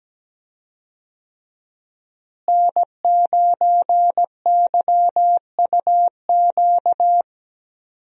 Morse code, heard as a keyed tone, about 17 wpm, 700 Hz. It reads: N9YUQ
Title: USA Call Signs - 5 Characters - Repeated 2x - 17wpm